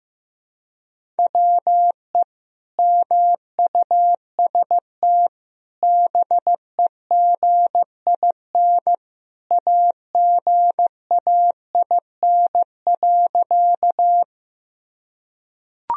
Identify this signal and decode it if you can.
Morse code, heard as a keyed tone, about 16 wpm, 700 Hz. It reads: WE MUST BEGIN AGAIN.
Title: Sentences from Top 1000 Words - Review Part 1 - 15wpm